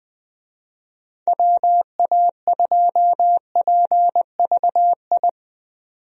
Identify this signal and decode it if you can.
Morse code, heard as a keyed tone, about 21 wpm, 700 Hz. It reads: WA2PVI
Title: USA Call Signs - 6 Characters - 20wpm